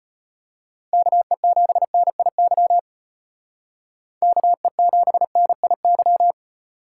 Morse code, heard as a keyed tone, about 36 wpm, 700 Hz. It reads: KE7NIY KE7NIY